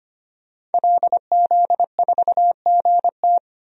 Morse code, heard as a keyed tone, about 25 wpm, 700 Hz. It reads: LZ4GT
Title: DXCC-100 Call Signs - 5 Characters - 25wpm